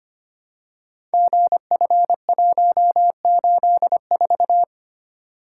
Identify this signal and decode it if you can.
Morse code, heard as a keyed tone, about 25 wpm, 700 Hz. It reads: GF184
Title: Code Groups - 5 Letter:Numbers - 25wpm